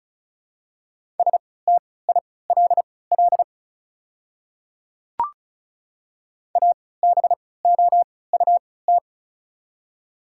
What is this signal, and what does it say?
Morse code, heard as a keyed tone, about 36 wpm, spaced as for 20 wpm, 700 Hz. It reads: STILL ABOUT
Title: 3x Character Spacing - Top 300 Words - 35wpm